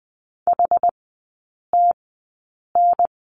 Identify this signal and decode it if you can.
Morse code, heard as a keyed tone, about 20 wpm, spaced as for 8 wpm, 700 Hz. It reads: HTN
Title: 067 Lesson 12e - Letter d - Three Characters 20wpm